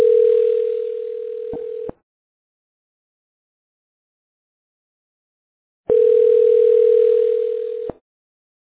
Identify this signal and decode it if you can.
Ringback tone (North America), 440 + 480 Hz.